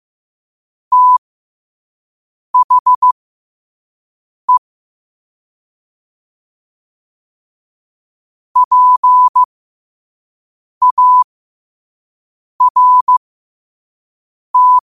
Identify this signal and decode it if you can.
Morse code, heard as a keyed tone, about 15 wpm, spaced as for 5 wpm, 1000 Hz. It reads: THE PART
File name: qotdfiles / 5wpm